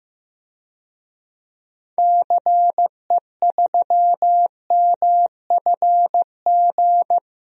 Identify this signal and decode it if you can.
Morse code, heard as a keyed tone, about 15 wpm, 700 Hz. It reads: CE3MFG
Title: DXCC-100 Call Signs - 6 Characters - 15wpm